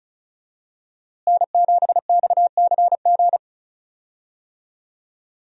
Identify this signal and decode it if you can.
Morse code, heard as a keyed tone, about 36 wpm, 700 Hz. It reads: N7XCG
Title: USA Call Signs - 5 Characters - Repeated 3x - 35wpm